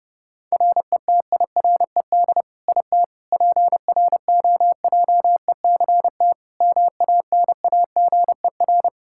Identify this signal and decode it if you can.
Morse code, heard as a keyed tone, about 30 wpm, 700 Hz. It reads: RETIRED IT PROJECT MANAGER
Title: Practice QSO - Exchange 3 - More Info - 30wpm